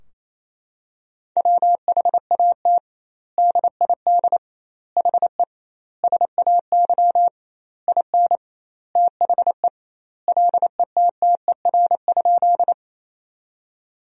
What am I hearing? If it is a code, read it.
Morse code, heard as a keyed tone, about 28 wpm, 700 Hz. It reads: WHAT DID HE SAY IN THE LETTER?
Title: 2x Word-Spacing - Sentences from Top 300 Words - 28wpm